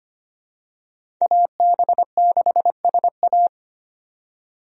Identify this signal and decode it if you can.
Morse code, heard as a keyed tone, about 25 wpm, 700 Hz. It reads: AB6SA
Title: USA Call Signs - 5 Characters - 25wpm